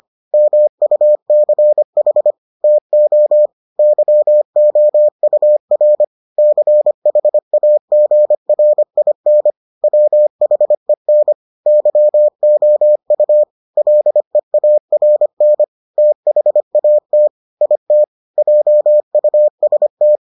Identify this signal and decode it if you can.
Morse code, heard as a keyed tone, about 25 wpm, 600 Hz. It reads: MUCH TO YOUR CHAGRIN WHEN YOU LEARN THAT IT JUST